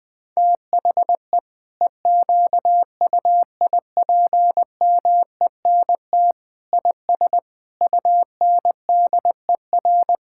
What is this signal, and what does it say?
Morse code, heard as a keyed tone, about 20 wpm, 700 Hz. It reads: THE EQUIPMENT IS UNDER
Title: Sentences from Top 1000 Words - Review Part 5 - 20wpm